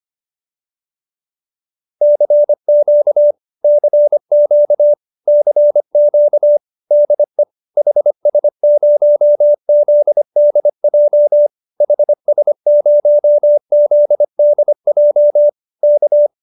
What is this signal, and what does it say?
Morse code, heard as a keyed tone, about 25 wpm, 590 Hz. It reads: CQ CQ CQ DE HS0ZDJ HS0ZDJ K